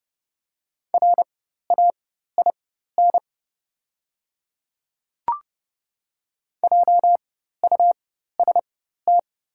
Morse code, heard as a keyed tone, about 30 wpm, spaced as for 14 wpm, 700 Hz. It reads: RAIN JUST
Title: 4x Character Spacing - Top 300 Words - 30wpm